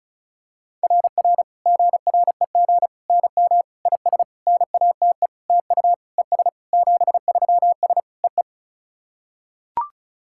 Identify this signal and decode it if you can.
Morse code, heard as a keyed tone, about 34 wpm, 690 Hz. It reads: RR GREG NM IS NATE TU ES 73S EE
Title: SOTA QSO Practice 35wpm